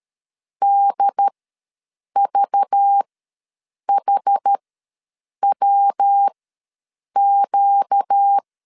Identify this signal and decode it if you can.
Morse code, heard as a keyed tone, about 13 wpm, spaced as for 7 wpm, 800 Hz. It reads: DVHWQ
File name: LEZIONI / 5 LBVC / 5 LBV ws 12wpm 3015